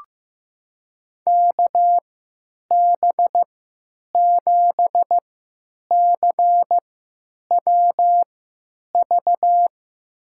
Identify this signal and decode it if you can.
Morse code, heard as a keyed tone, about 15 wpm, spaced as for 9 wpm, 700 Hz. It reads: KB7CWV
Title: USA Call Signs - 3x Character Spacing - 15wpm